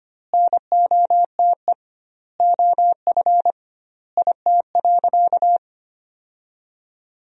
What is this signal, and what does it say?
Morse code, heard as a keyed tone, about 25 wpm, 700 Hz. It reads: NOTE OF IT.
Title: 2x Word-Spacing - Sentences from Top 500 Words - 25wpm